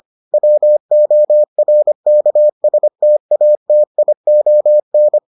Morse code, heard as a keyed tone, about 25 wpm, 600 Hz. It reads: WORKSTATION